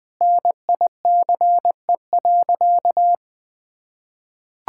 Morse code, heard as a keyed tone, about 19 wpm, 700 Hz. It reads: NICE.
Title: Sentences with 3-Word Phrases - Spoken Beginning - 20wpm